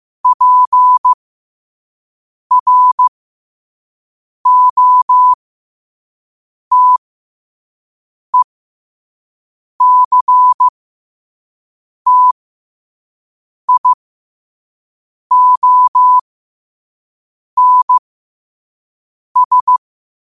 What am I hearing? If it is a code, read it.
Morse code, heard as a keyed tone, about 15 wpm, spaced as for 5 wpm, 1000 Hz. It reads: PROTECTIONS
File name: qotdfiles / 5wpm